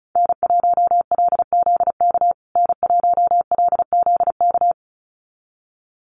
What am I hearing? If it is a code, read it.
Morse code, heard as a keyed tone, about 34 wpm, 700 Hz. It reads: N1LZK N1LZK
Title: USA Call Signs Repeated 2x 35wpm